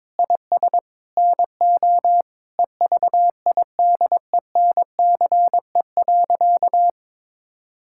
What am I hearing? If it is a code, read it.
Morse code, heard as a keyed tone, about 22 wpm, 700 Hz. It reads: IS NO EVIDENCE.
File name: practice / 22wpm / Sentences with 4-Word Phrases - 22wpm